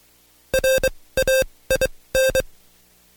Morse code, heard as a keyed tone, about 23 wpm, spaced as for 17 wpm, 530 Hz. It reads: RAIN
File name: Code Quick / AudioDownld2 / Seg135-Stepper8 17WPM